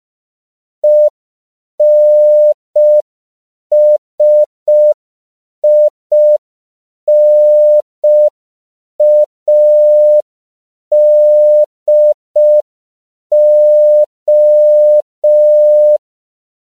Morse code, heard as a keyed tone, about 5 wpm, 600 Hz. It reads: ENSINADO
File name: CW / 5PPM